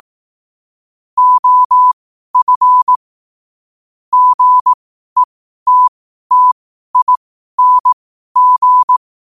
Morse code, heard as a keyed tone, about 18 wpm, spaced as for 12 wpm, 1000 Hz. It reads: OF GETTING
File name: qotdfiles / 12wpm